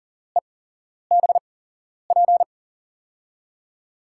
Morse code, heard as a keyed tone, about 40 wpm, spaced as for 11 wpm, 700 Hz. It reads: EBP
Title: USA Call Signs - 6 Characters - 8x Character Spacing - 40wpm